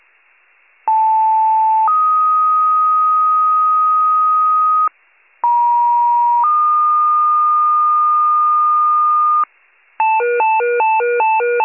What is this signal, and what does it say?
Two-tone page, heard as a one-second tone, then a three-second tone, about 880 then 1250 Hz, about 950 then 1220 Hz. It is heard from 0.9 to 9.4 s.
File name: pager / 76 FIRE / wcalert